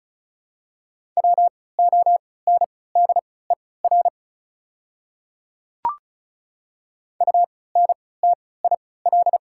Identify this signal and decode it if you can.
Morse code, heard as a keyed tone, about 34 wpm, spaced as for 20 wpm, 700 Hz. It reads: WONDER UNTIL
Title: 3x Character Spacing - Top 500 Words - 35wpm